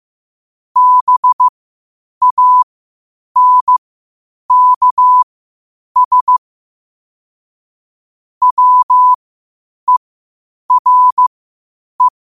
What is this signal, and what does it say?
Morse code, heard as a keyed tone, about 15 wpm, spaced as for 8 wpm, 1000 Hz. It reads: BANKS WERE
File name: qotdfiles / 8wpm